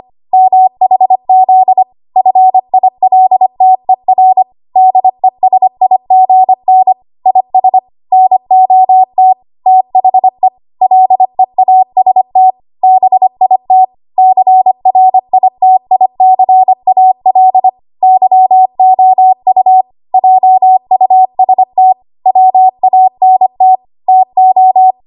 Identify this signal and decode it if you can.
Morse code, heard as a keyed tone, about 25 wpm, 750 Hz. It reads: MHZ FILTER DESIGN IS NOT THE LEAST BIT CRITICAL YOU JUST WANT TO
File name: Archive / 25 WPM / 25WPM